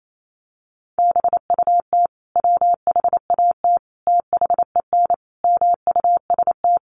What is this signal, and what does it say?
Morse code, heard as a keyed tone, about 28 wpm, 700 Hz. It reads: BUT WHAT THEN MUST
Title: Sentences from Top 500 Words 28wpm